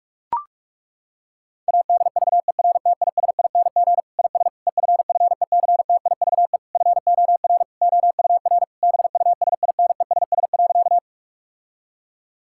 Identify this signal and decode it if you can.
Morse code, heard as a keyed tone, about 41 wpm, 700 Hz. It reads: ADVERTISING IS EFFECTIVE FOR OUR BUSINESS.
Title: Sentences from Top 1000 Words - Review Part 6 - 45wpm